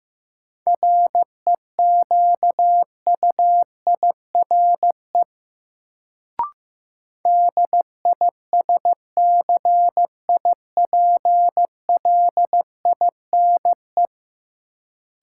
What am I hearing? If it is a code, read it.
Morse code, heard as a keyed tone, about 15 wpm, 700 Hz. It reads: REQUIRE DISCIPLINE
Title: Sentences from Top 900 Words - New Word Drill - 15wpm